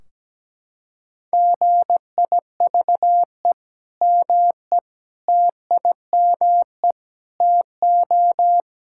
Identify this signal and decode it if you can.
Morse code, heard as a keyed tone, about 17 wpm, 700 Hz. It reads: GIVE ME TIME TO
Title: Sentences from Top 1600 Words 17wpm